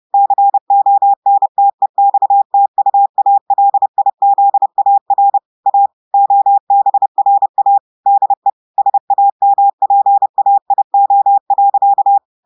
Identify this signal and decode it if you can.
Morse code, heard as a keyed tone, about 30 wpm, 800 Hz. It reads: CONTEXTUALIZAR A OBRA DE SAMPAIO.
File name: audios / portugais / portugais texte en clair 1 vitesse 30 mots minutes